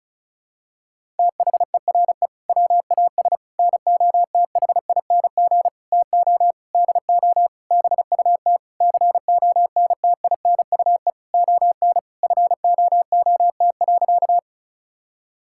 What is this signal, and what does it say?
Morse code, heard as a keyed tone, about 34 wpm, 700 Hz. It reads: THERE WAS NOTHING TO DO BUT CONTINUE ON FOOT.
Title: Sentences from Top 1000 Words - Review Part 6 - 35wpm